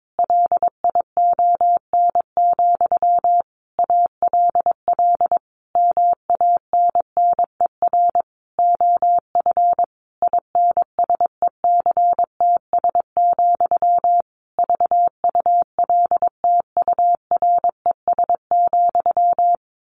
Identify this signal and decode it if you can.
Morse code, heard as a keyed tone, about 22 wpm, 700 Hz. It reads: LION, ALL MANNER OF INSECTS, VULTURES,
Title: A Sound Of Thunder 22wpm